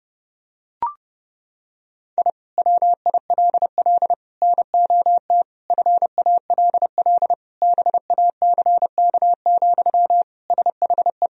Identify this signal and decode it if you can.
Morse code, heard as a keyed tone, about 32 wpm, 700 Hz. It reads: I WILL NOT FALL BACK, SHE
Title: Sentences from Top 1000 Words - Review Part 2 - 30wpm